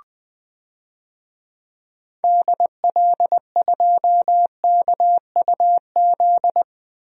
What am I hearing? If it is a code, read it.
Morse code, heard as a keyed tone, about 20 wpm, 700 Hz. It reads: DL2KUZ
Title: DXCC-100 Call Signs - 6 Characters - 20wpm